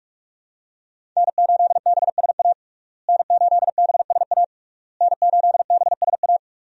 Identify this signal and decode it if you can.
Morse code, heard as a keyed tone, about 46 wpm, 700 Hz. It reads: N8BSA N8BSA N8BSA
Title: USA Call Signs - 5 Characters - Repeated 3x - 45wpm